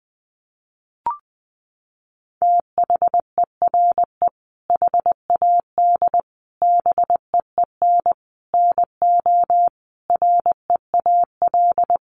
Morse code, heard as a keyed tone, about 20 wpm, 700 Hz. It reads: THERE HAD BEEN NO REAL
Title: Sentences from Top 1400 Words 20wpm